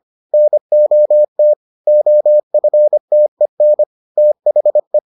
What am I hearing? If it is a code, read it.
Morse code, heard as a keyed tone, about 25 wpm, 600 Hz. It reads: NOT OFTEN THE